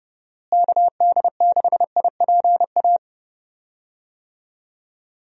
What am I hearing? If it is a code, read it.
Morse code, heard as a keyed tone, about 30 wpm, 700 Hz. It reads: KD6IPA
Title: USA Call Signs - 6 Characters - Repeated 2x - 30wpm